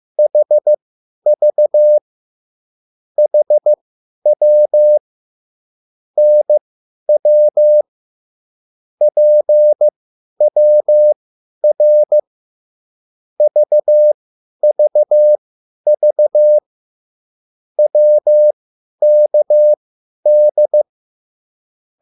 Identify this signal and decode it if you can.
Morse code, heard as a keyed tone, about 15 wpm, spaced as for 10 wpm, 600 Hz. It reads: HV HW NW PWR VVV WKD
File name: QSL / CW lesson